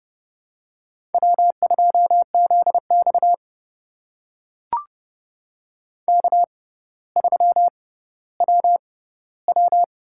Morse code, heard as a keyed tone, about 30 wpm, spaced as for 15 wpm, 700 Hz. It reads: W2ZX K3WW
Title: USA Call Signs - 4 Characters - 6x Character Spacing - 30wpm